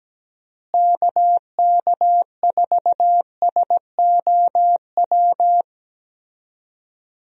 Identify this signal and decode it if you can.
Morse code, heard as a keyed tone, about 17 wpm, 700 Hz. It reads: KK4SOW
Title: USA Call Signs - 6 Characters - Repeated 2x - 17wpm